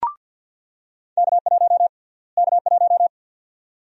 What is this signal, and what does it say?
Morse code, heard as a keyed tone, about 50 wpm, 700 Hz. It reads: K1 K1
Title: USA Call Sign Prefixes - Repeated 2x - 50wpm